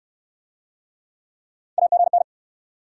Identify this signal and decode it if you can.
Morse code, heard as a keyed tone, about 66 wpm, 700 Hz. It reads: AGN